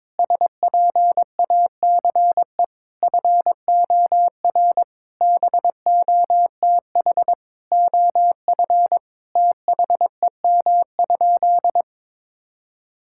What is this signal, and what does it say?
Morse code, heard as a keyed tone, about 21 wpm, 700 Hz. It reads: SPACE FOR BOTH OF THEM?
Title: Sentences from Top 1000 Words - Review Part 5 - 22wpm